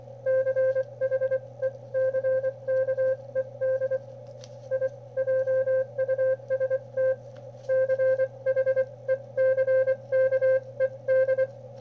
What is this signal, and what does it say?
Morse code, heard as a keyed tone, about 25 wpm, spaced as for 21 wpm, 520 Hz. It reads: CHECKED I JUST CHECKED
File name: qrp / wb0jnr at 400mW with ae5x 02-23-2018